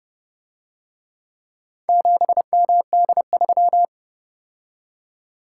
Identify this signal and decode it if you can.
Morse code, heard as a keyed tone, about 30 wpm, 700 Hz. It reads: 7MD3